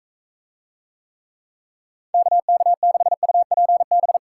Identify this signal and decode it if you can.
Morse code, heard as a keyed tone, about 40 wpm, 700 Hz. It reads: KK6UPB